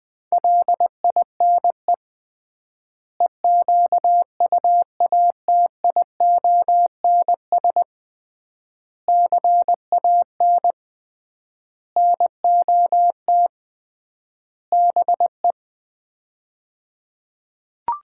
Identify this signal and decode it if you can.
Morse code, heard as a keyed tone, about 20 wpm, 700 Hz. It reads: LINE EQUATIONS CAN NOT BE
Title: 3x Word-Spacing - Sets of 5 Words - 20wpm